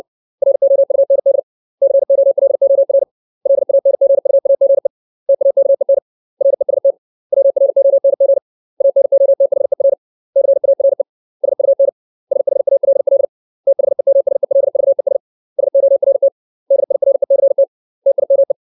Morse code, heard as a keyed tone, about 60 wpm, 550 Hz. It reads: ROUND COLOR LANGUAGE TAKEN REST WRONG ANOTHER CARE SUN SHALL THEMSELVES SORT DIRECT TIME
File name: 632 words / 632 Words 60WPM